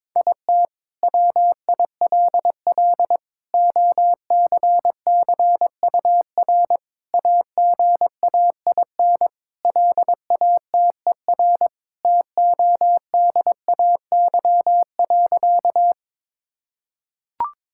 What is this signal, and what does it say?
Morse code, heard as a keyed tone, about 22 wpm, 700 Hz. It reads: IT WILL OCCUR AGAIN LATER TODAY.
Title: Sentences from Top 1000 Words - Review Part 6 - 22wpm